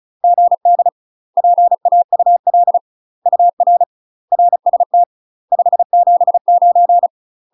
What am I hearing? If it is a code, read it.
Morse code, heard as a keyed tone, about 34 wpm, 700 Hz. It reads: GD PAUL UR RST 579
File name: UK MASTER QSOs TRAINING / 35WPM / UK MASTER QSOs 0021